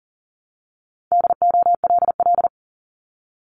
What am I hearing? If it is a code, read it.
Morse code, heard as a keyed tone, about 41 wpm, 700 Hz. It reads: DOLL
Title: Multi-letter Phonemes 40wpm